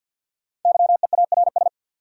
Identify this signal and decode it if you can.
Morse code, heard as a keyed tone, about 53 wpm, 700 Hz. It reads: YEARS
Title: Sentences from Top 200 Words - New Word Drill - 50wpm